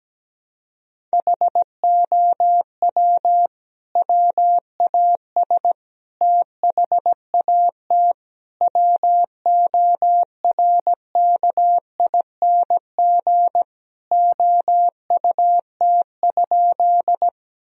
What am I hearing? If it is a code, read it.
Morse code, heard as a keyed tone, about 17 wpm, 700 Hz. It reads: HOW WAS THAT WORKING OUT?
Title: Sentences from Top 1000 Words - Review Part 4 - 17wpm